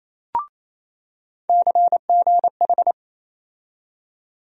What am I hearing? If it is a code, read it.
Morse code, heard as a keyed tone, about 28 wpm, 700 Hz. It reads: CGH
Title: USA Call Sign Postfixes - 3 Letters 28wpm